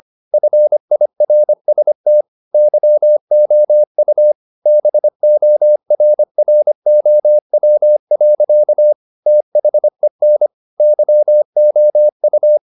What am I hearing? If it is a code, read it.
Morse code, heard as a keyed tone, about 25 wpm, 600 Hz. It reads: FIRST YOU BORROW. THEN YOU
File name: the old man and the sea part 1 25wpm